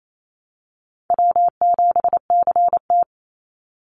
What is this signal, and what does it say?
Morse code, heard as a keyed tone, about 28 wpm, 700 Hz. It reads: W7CT
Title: Call Signs - 28wpm